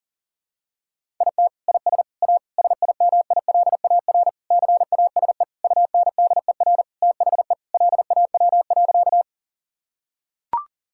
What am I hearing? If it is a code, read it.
Morse code, heard as a keyed tone, about 40 wpm, 700 Hz. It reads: IT IS A SIMILAR CASE UNDER THE LAW.
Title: Sentences from Top 1000 Words - Review Part 5 - 40wpm